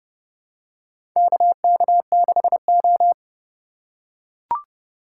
Morse code, heard as a keyed tone, about 32 wpm, 700 Hz. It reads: KK6O
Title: USA Call Signs - 4 Characters - 4x Character Spacing - 30wpm